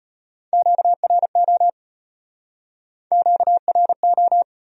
Morse code, heard as a keyed tone, about 32 wpm, 700 Hz. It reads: QRO QRO